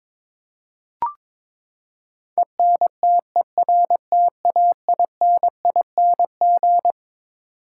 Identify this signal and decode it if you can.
Morse code, heard as a keyed tone, about 22 wpm, 700 Hz. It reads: ENTERTAINING
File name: Twelve Letter Words - 22wpm